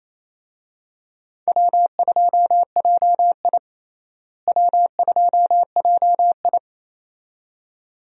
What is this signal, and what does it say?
Morse code, heard as a keyed tone, about 28 wpm, 700 Hz. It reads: W2JI W2JI
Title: USA Call Signs - 4 Characters - Repeated 2x - 28wpm